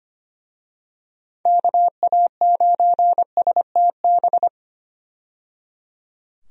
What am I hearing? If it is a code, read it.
Morse code, heard as a keyed tone, about 25 wpm, 700 Hz. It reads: KA9STB
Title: USA Call Signs - 6 Characters - 25wpm